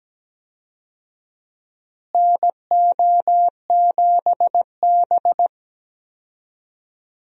Morse code, heard as a keyed tone, about 17 wpm, 700 Hz. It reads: NO7B
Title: USA Call Signs - 4 Characters - Repeated 2x - 17wpm